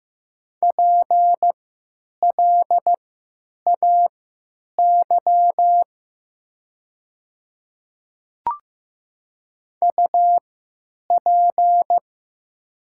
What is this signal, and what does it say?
Morse code, heard as a keyed tone, about 15 wpm, spaced as for 9 wpm, 700 Hz. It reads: PLAY UP
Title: 3x Character Spacing - Top 200 Words - 15wpm